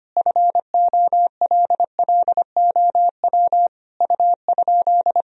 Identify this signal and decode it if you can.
Morse code, heard as a keyed tone, about 25 wpm, 700 Hz. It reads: FOLLOW U?
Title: Sentences from Top 200 Words and Abbreviations 25wpm